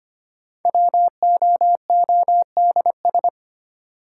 Morse code, heard as a keyed tone, about 25 wpm, 700 Hz. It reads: WOODS
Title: Sentences from Top 1000 Words - Review Part 2 - New Word Drill - 25wpm